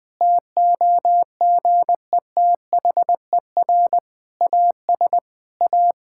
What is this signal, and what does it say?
Morse code, heard as a keyed tone, about 21 wpm, 700 Hz. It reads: TOGETHER AS A
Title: Difficult Sentences with Hints - 20wpm